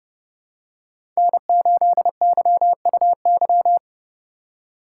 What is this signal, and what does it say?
Morse code, heard as a keyed tone, about 30 wpm, 700 Hz. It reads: N8YUY
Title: USA Call Signs - 5 Characters - 2x Character Spacing - 30wpm